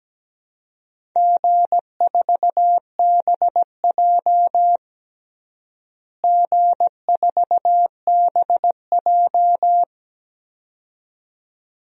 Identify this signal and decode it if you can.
Morse code, heard as a keyed tone, about 17 wpm, 700 Hz. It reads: G4BJ G4BJ
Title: DXCC-100 Call Signs - 4 Characters - Repeated 2x - 17wpm